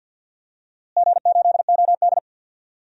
Morse code, heard as a keyed tone, about 50 wpm, 700 Hz. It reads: G8OD
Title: DXCC-100 Call Signs - 4 Characters - Repeated 3x - 50wpm